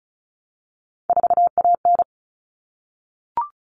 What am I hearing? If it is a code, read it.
Morse code, heard as a keyed tone, about 34 wpm, 700 Hz. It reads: 4AN
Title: All Cut Numbers - 3-Digit 35wpm